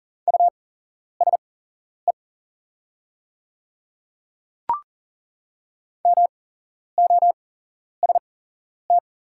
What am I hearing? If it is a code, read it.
Morse code, heard as a keyed tone, about 38 wpm, spaced as for 11 wpm, 700 Hz. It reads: USE MOST
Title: 8x Character Spacing - Top 500 Words - 40wpm